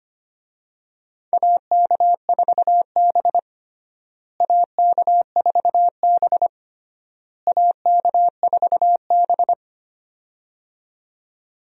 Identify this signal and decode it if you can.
Morse code, heard as a keyed tone, about 25 wpm, 700 Hz. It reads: AK4B AK4B AK4B
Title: USA Call Signs - 4 Characters - Repeated 3x - 25wpm